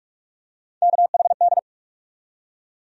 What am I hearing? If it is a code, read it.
Morse code, heard as a keyed tone, about 41 wpm, 700 Hz. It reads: KHD
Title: USA Call Sign Postfixes - 3 Letters 45wpm